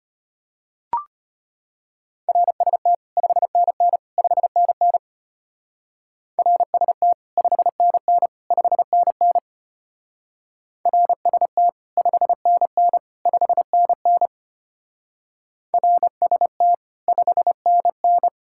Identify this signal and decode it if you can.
Morse code, heard as a keyed tone, about 30 wpm, 700 Hz. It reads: RST 5NN 5NN RST 5NN 5NN RST 5NN 5NN RST 5NN
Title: Speed-Racing Common QSO Phrases - 25wpm